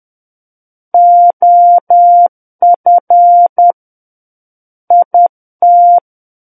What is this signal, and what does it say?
Morse code, heard as a keyed tone, about 10 wpm, 700 Hz. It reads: OF IT
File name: Book One / 01-Chapter 1 - 10 WPM